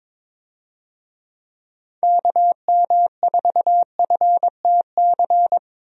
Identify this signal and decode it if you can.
Morse code, heard as a keyed tone, about 22 wpm, 700 Hz. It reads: KM4FTC
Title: USA Call Signs - 6 Characters - Repeated 3x - 22wpm